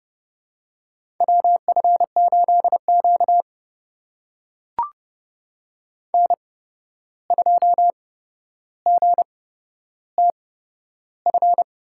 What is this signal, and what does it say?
Morse code, heard as a keyed tone, about 30 wpm, spaced as for 12 wpm, 700 Hz. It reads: WF8Q N2GTF
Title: USA Call Signs - 8x Character Spacing - 30wpm